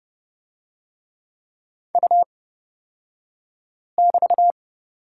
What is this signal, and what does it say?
Morse code, heard as a keyed tone, about 30 wpm, spaced as for 5 wpm, 700 Hz. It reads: U<BT>
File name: Lesson 36d - BT - Two Characters 30wpm